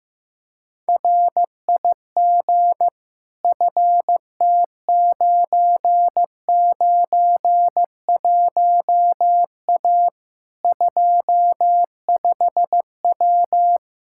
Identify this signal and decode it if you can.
Morse code, heard as a keyed tone, about 15 wpm, 700 Hz. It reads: RIG FT991A 25W
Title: Practice QSO - Exchange 2 - Info - 15wpm